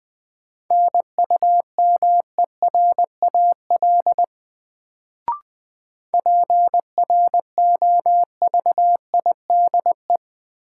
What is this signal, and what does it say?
Morse code, heard as a keyed tone, about 20 wpm, 700 Hz. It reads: NUMERAL PROVIDE
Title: Sentences from Top 1000 Words - Review Part 2 - New Word Drill - 20wpm